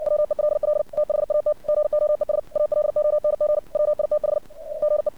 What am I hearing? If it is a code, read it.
Morse code, heard as a keyed tone, about 57 wpm, 600 Hz. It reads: WELL THAT GOES ALONG WITH GE
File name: archived / CRSnet